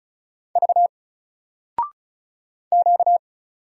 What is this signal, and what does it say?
Morse code, heard as a keyed tone, about 35 wpm, 700 Hz. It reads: V Q
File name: Single-Letter-Number-35wpm